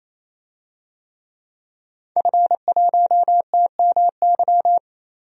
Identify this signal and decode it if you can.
Morse code, heard as a keyed tone, about 27 wpm, 690 Hz. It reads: F1TMY
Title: DXCC-100 Call Signs - 5 Characters - 28wpm